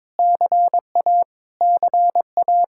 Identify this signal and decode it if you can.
Morse code, heard as a keyed tone, about 22 wpm, 700 Hz. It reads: CA CA
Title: Practice QSO - Exchange 1 - Beginning - 22wpm